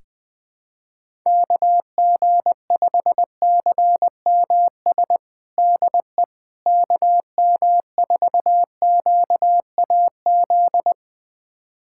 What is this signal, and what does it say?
Morse code, heard as a keyed tone, about 20 wpm, 700 Hz. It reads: KG5CMS DE KM4QAZ
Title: USA Callsign DE Callsign - 6 Characters - 20wpm